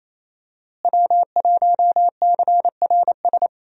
Morse code, heard as a keyed tone, about 28 wpm, 700 Hz. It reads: W1CRS
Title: USA Call Signs - 5 Characters - Repeated 3x - 28wpm